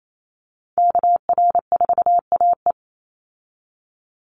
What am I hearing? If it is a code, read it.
Morse code, heard as a keyed tone, about 28 wpm, 700 Hz. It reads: KR4AE
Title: Call Signs Repeated 2x - 28wpm